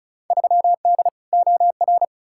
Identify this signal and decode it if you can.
Morse code, heard as a keyed tone, about 36 wpm, 700 Hz. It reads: 3D OR
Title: Field Day Exchanges Encore - 35wpm